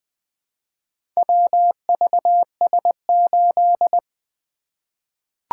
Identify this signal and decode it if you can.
Morse code, heard as a keyed tone, about 21 wpm, 700 Hz. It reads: WVS8
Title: Four Letter-Number Combinations - 20wpm